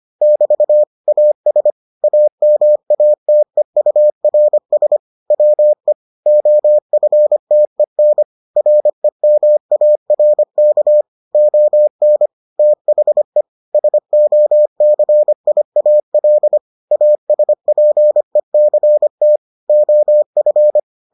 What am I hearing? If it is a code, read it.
Morse code, heard as a keyed tone, about 25 wpm, 600 Hz. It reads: <BT> AS AMATEURS WE OFTEN REMARK ON THE SOCIAL ASPECT OF